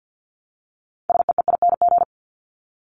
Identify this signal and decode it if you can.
Morse code, heard as a keyed tone, about 50 wpm, 700 Hz. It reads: SEEING
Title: ING Suffix - Top 100 Words with Base Words 50wpm